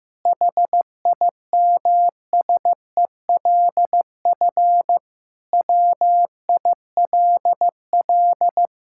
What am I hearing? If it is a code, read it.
Morse code, heard as a keyed tone, about 15 wpm, 700 Hz. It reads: HIMSELF WILL
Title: Sentences from Top 1600 Words 15wpm